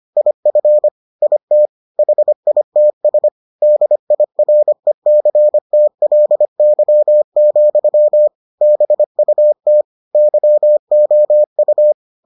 Morse code, heard as a keyed tone, about 25 wpm, 600 Hz. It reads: IF IT HITS DIRECTLY, BUT YOU